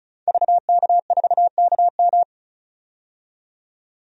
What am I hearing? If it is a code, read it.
Morse code, heard as a keyed tone, about 34 wpm, 700 Hz. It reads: VK4KM
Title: DXCC-100 Call Signs - 5 Characters - 35wpm